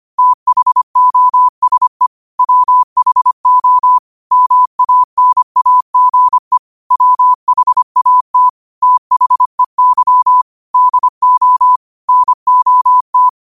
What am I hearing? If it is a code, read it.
Morse code, heard as a keyed tone, about 25 wpm, 1000 Hz. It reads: THOSE WHO MANAGE WHAT THEY DO NOT
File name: 25wpm